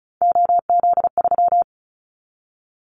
Morse code, heard as a keyed tone, about 34 wpm, 700 Hz. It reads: OZ3
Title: Callsign Prefixes for Top 100 DXCC - 35wpm